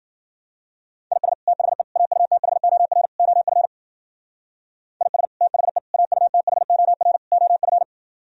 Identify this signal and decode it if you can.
Morse code, heard as a keyed tone, about 51 wpm, 700 Hz. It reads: IS THE AUTHOR OF IS THE AUTHOR OF